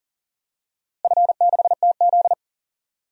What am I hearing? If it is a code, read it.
Morse code, heard as a keyed tone, about 40 wpm, 700 Hz. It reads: F6TZ
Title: DXCC-100 Call Signs - 4 Characters - Repeated 2x - 40wpm